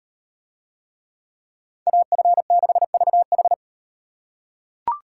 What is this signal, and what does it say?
Morse code, heard as a keyed tone, about 38 wpm, 700 Hz. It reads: AF6VH